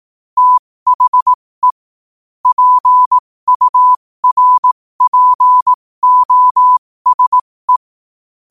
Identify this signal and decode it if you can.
Morse code, heard as a keyed tone, about 18 wpm, spaced as for 15 wpm, 1000 Hz. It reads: THE PURPOSE
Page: New quotes every day in morse code at 15 Words per minute.